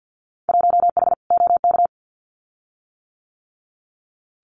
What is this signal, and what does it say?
Morse code, heard as a keyed tone, about 48 wpm, 700 Hz. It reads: 1H OK
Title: Winter Field Day Exchanges 50wpm